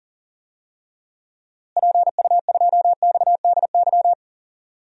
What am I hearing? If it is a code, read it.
Morse code, heard as a keyed tone, about 40 wpm, 700 Hz. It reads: PU2XDY
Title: DXCC-100 Call Signs - 6 Characters - Repeated 2x - 40wpm